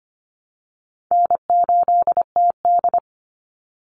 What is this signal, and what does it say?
Morse code, heard as a keyed tone, about 25 wpm, 700 Hz. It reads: N8TD
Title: Easy Call Signs Repeated 2x - 25wpm